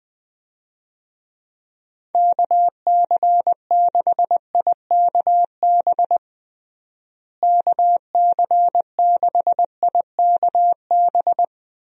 Morse code, heard as a keyed tone, about 20 wpm, 700 Hz. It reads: KC6IKB KC6IKB
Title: USA Call Signs - 6 Characters - Repeated 3x - 20wpm